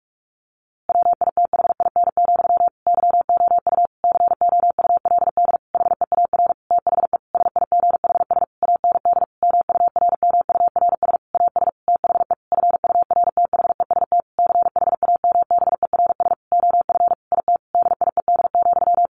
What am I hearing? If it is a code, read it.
Morse code, heard as a keyed tone, about 46 wpm, 700 Hz. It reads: WITHIN, YOU COULD HEAR THE SIGHS AND MURMURS AS THE FURTHEST CHAMBERS OF IT DIED,
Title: A Sound Of Thunder 45wpm